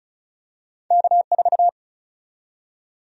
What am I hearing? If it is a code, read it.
Morse code, heard as a keyed tone, about 34 wpm, 700 Hz. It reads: K4
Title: USA Call Sign Prefixes - 35wpm